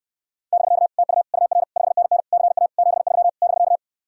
Morse code, heard as a keyed tone, about 67 wpm, 700 Hz. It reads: <BK> TU UR 5NN CA 73 <BK>